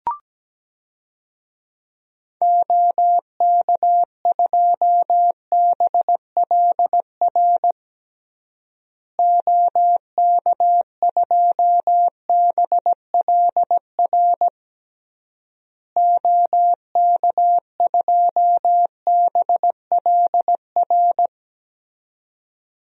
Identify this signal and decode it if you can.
Morse code, heard as a keyed tone, about 17 wpm, 700 Hz. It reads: OK2BLR OK2BLR OK2BLR
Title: DXCC-100 Call Signs - 6 Characters - Repeated 3x - 17wpm